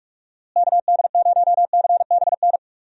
Morse code, heard as a keyed tone, about 43 wpm, 700 Hz. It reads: KD0CBN
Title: USA Call Signs - 6 Characters - Repeated 3x - 45wpm